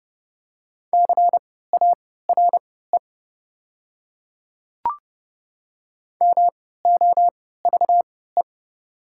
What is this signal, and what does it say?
Morse code, heard as a keyed tone, about 30 wpm, spaced as for 17 wpm, 700 Hz. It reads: CARE MOVE
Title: 3x Character Spacing - Top 300 Words - 30wpm